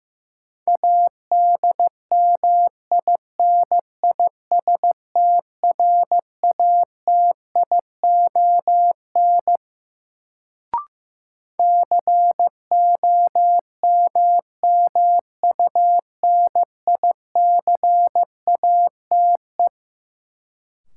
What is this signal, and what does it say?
Morse code, heard as a keyed tone, about 15 wpm, 700 Hz. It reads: ADMINISTRATION COMMUNICATE
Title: Sentences from Top 1300 Words - New Word Drill Alt - 15wpm